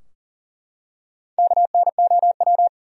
Morse code, heard as a keyed tone, about 40 wpm, 700 Hz. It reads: KNOW
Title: Top 100 Words Repeated 40wpm